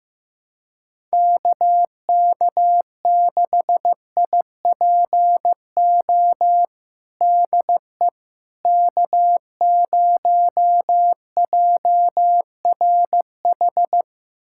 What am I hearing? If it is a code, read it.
Morse code, heard as a keyed tone, about 15 wpm, 700 Hz. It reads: KK6IPO DE K0JRH
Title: USA Callsign DE Callsign 15wpm